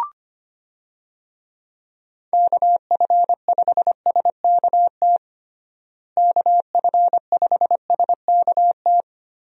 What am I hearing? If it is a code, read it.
Morse code, heard as a keyed tone, about 25 wpm, 700 Hz. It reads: KF5SKT KF5SKT
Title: USA Call Signs - 6 Characters - Repeated 2x - 25wpm